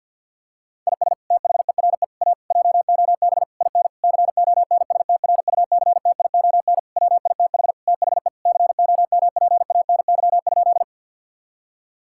Morse code, heard as a keyed tone, about 50 wpm, 700 Hz. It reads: IS THERE A JOB IN CONSTRUCTION WITH THE COMPANY?
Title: Sentences from Top 1600 Words 50wpm